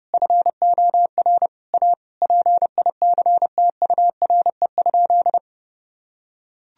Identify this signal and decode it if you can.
Morse code, heard as a keyed tone, about 32 wpm, 700 Hz. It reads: FOR A PICTURE?
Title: Sentences from Top 1000 Words - Review Part 2 - 30wpm